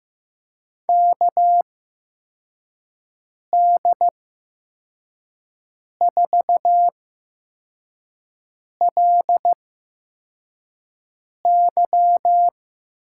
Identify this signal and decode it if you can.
Morse code, heard as a keyed tone, about 15 wpm, spaced as for 4 wpm, 700 Hz. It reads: KD4LY
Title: USA Call Signs - 8x Character Spacing - 15wpm